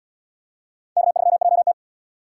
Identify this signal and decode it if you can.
Morse code, heard as a keyed tone, about 77 wpm, 700 Hz. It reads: K3JT